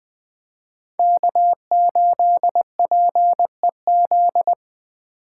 Morse code, heard as a keyed tone, about 21 wpm, 700 Hz. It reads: K8PEZ